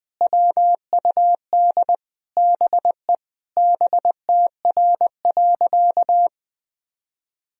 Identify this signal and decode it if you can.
Morse code, heard as a keyed tone, about 21 wpm, 700 Hz. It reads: WUD BE BTR.
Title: Sentences from Top 400 Words and Abbreviations 20wpm